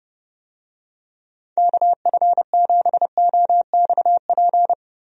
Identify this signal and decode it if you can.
Morse code, heard as a keyed tone, about 30 wpm, 700 Hz. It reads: KF7OXP
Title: USA Call Signs - 6 Characters - 30wpm